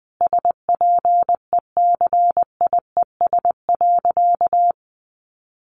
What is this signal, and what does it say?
Morse code, heard as a keyed tone, about 20 wpm, 700 Hz. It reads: SPECIES.
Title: A Sound Of Thunder 20wpm